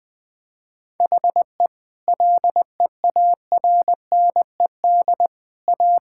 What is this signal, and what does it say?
Morse code, heard as a keyed tone, about 20 wpm, 700 Hz. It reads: HE LEARNED A
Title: Sentences with 3-Word Phrases - 20wpm